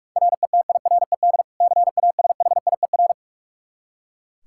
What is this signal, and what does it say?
Morse code, heard as a keyed tone, about 41 wpm, 700 Hz. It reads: RETIRED CASHIER
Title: K1USN SST QSO Practice 45wpm